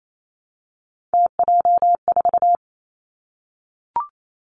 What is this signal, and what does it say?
Morse code, heard as a keyed tone, about 28 wpm, 700 Hz. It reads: TJ4
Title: Three Challenging Letter-Number Combinations 28wpm